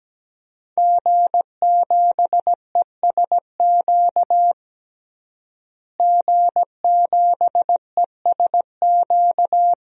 Morse code, heard as a keyed tone, about 17 wpm, 700 Hz. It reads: G7ESQ G7ESQ
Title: DXCC-100 Call Signs - 5 Characters - Repeated 3x - 17wpm